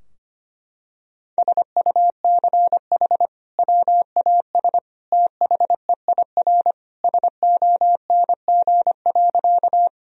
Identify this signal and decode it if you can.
Morse code, heard as a keyed tone, about 25 wpm, 700 Hz. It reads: SUCH WAS THEIR SONG.
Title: Sentences from Top 1000 Words - Review Part 2 - 25wpm